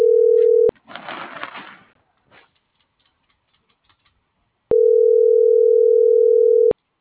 Ringback tone (North America), 440 + 480 Hz.